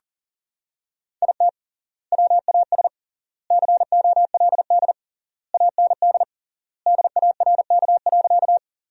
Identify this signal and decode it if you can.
Morse code, heard as a keyed tone, about 40 wpm, 700 Hz. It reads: IT WAS COLD AND DARK.
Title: 3x Word-Spacing - Sentences from Top 500 Words - 40wpm